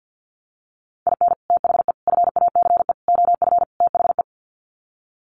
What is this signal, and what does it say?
Morse code, heard as a keyed tone, about 50 wpm, 700 Hz. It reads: IN THE FACE OF THE
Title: Sets of 5 Words 50wpm